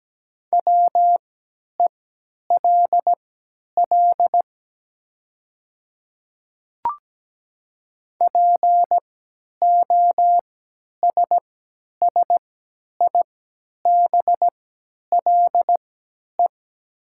Morse code, heard as a keyed tone, about 17 wpm, spaced as for 10 wpm, 700 Hz. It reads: WELL POSSIBLE
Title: 3x Character Spacing - Top 500 Words - 17wpm